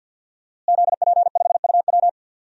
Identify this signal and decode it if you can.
Morse code, heard as a keyed tone, about 52 wpm, 700 Hz. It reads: ZP5UW
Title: DXCC-100 Call Signs - 5 Characters - Repeated 2x - 50wpm